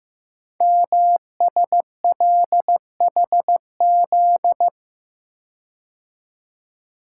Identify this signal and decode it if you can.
Morse code, heard as a keyed tone, about 15 wpm, 700 Hz. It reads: MSLHZ
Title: Code Groups - 5 Letter:Numbers - 15wpm